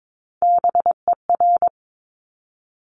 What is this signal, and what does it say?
Morse code, heard as a keyed tone, about 22 wpm, 700 Hz. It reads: BER
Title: Top Three Letter Combinations in Words 22wpm